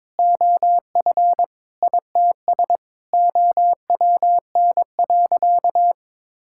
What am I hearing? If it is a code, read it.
Morse code, heard as a keyed tone, about 22 wpm, 700 Hz. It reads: OF ITS OWN.
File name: practice / 22wpm / Sentences with 8-Word Phrases - 22wpm